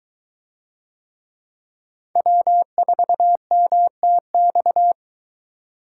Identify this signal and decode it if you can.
Morse code, heard as a keyed tone, about 22 wpm, 700 Hz. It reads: W4MTX